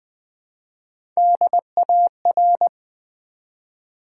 Morse code, heard as a keyed tone, about 20 wpm, 700 Hz. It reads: DAR
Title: Build A Word - 1 Letter At A Time - 4-Letter Words - 20wpm